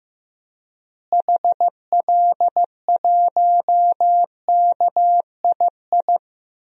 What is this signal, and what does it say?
Morse code, heard as a keyed tone, about 16 wpm, 700 Hz. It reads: HL1KII
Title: DXCC-100 Call Signs - 6 Characters - Repeated 2x - 15wpm